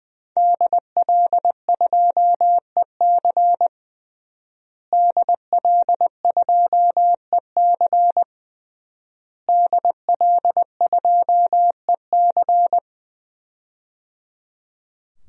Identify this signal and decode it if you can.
Morse code, heard as a keyed tone, about 20 wpm, 700 Hz. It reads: DL2EC DL2EC DL2EC
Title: DXCC-100 Call Signs - 5 Characters - Repeated 3x - 20wpm